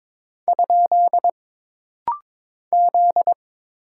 Morse code, heard as a keyed tone, about 22 wpm, 700 Hz. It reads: ? Z
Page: This rapid-fire timing is geared to push your ICR (Instant Character Recognition) skills.